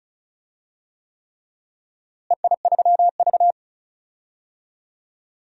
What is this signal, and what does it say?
Morse code, heard as a keyed tone, about 34 wpm, 700 Hz. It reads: EI3V
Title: DXCC-100 Call Signs - 4 Characters - Repeated 3x - 35wpm